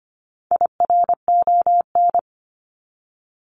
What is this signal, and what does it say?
Morse code, heard as a keyed tone, about 25 wpm, 700 Hz. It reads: IRON
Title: Top 1751-2000 Words 25wpm